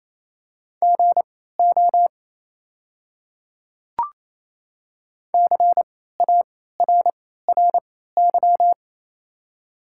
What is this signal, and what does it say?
Morse code, heard as a keyed tone, about 28 wpm, spaced as for 16 wpm, 700 Hz. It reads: GO CARRY
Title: 3x Character Spacing - Top 500 Words - 28wpm